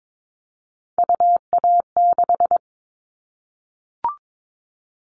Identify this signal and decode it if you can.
Morse code, heard as a keyed tone, about 22 wpm, 700 Hz. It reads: UA6
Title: All Cut Numbers - 3-Digit 22wpm